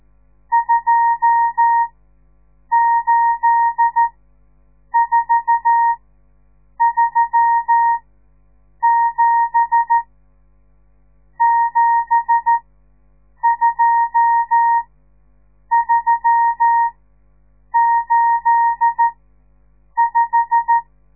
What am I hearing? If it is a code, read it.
Morse code, heard as a keyed tone, about 13 wpm, spaced as for 7 wpm, 930 Hz. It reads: 28437 72385
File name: radiomorningstar / morse / les 10